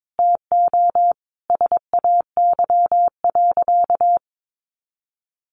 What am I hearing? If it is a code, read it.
Morse code, heard as a keyed tone, about 22 wpm, 700 Hz. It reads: TO SAY.
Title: Sentences from Top 600 Words 22wpm